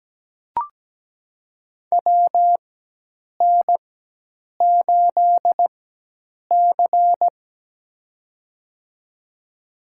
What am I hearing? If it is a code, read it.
Morse code, heard as a keyed tone, about 17 wpm, spaced as for 8 wpm, 700 Hz. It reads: WN8C
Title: USA Call Signs - 4 Characters - 4x Character Spacing - 17wpm